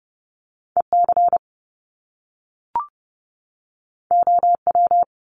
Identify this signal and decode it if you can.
Morse code, heard as a keyed tone, about 30 wpm, 700 Hz. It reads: EC OW
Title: Top Two Letter Combinations in Words 30wpm